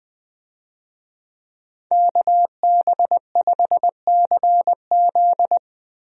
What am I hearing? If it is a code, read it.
Morse code, heard as a keyed tone, about 20 wpm, 700 Hz. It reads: KB5CZ
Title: USA Call Signs - 5 Characters - Repeated 2x - 20wpm